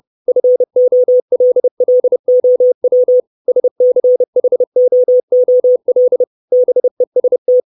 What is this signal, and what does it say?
Morse code, heard as a keyed tone, about 30 wpm, 500 Hz. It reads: FOLLOW SCHOOL BEST
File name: 632 Words 30WPM